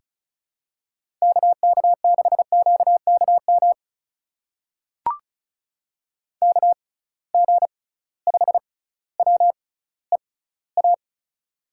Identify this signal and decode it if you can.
Morse code, heard as a keyed tone, about 36 wpm, spaced as for 18 wpm, 700 Hz. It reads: KK6QKM KG5WEA
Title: USA Call Signs - 6 Characters - 6x Character Spacing - 35wpm